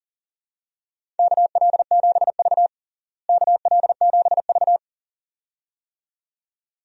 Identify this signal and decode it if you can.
Morse code, heard as a keyed tone, about 40 wpm, 700 Hz. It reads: KL7V KL7V
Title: DXCC-100 Call Signs - 4 Characters - Repeated 2x - 40wpm